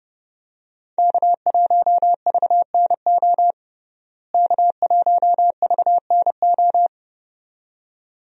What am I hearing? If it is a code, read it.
Morse code, heard as a keyed tone, about 31 wpm, 700 Hz. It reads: K1VNO K1VNO
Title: USA Call Signs - 5 Characters - Repeated 2x - 30wpm